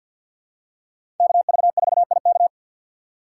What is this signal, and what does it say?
Morse code, heard as a keyed tone, about 50 wpm, 700 Hz. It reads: KV4IK
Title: USA Call Signs - 5 Characters - 3x Character Spacing - 50wpm